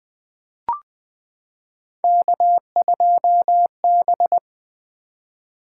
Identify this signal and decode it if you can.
Morse code, heard as a keyed tone, about 20 wpm, 700 Hz. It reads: K2B
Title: Three Letter-Number Combinations - 20wpm